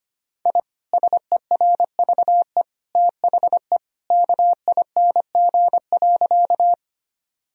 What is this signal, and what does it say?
Morse code, heard as a keyed tone, about 25 wpm, 700 Hz. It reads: I SERVE THE KING.
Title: Sentences from Top 1000 Words - Review Part 2 - 25wpm